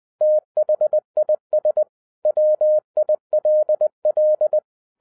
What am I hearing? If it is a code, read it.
Morse code, heard as a keyed tone, about 20 wpm, 610 Hz. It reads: THIS WILL